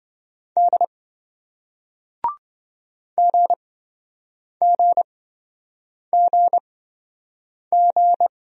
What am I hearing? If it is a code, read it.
Morse code, heard as a keyed tone, about 26 wpm, spaced as for 7 wpm, 700 Hz. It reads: D GGGG